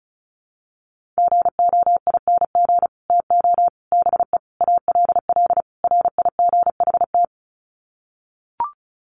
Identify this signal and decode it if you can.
Morse code, heard as a keyed tone, about 34 wpm, 700 Hz. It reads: GOING TO BE ALL RIGHT
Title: Sets of 5 Words Encore2 35wpm